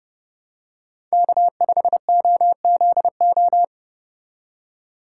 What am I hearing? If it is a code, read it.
Morse code, heard as a keyed tone, about 31 wpm, 700 Hz. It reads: K5OZO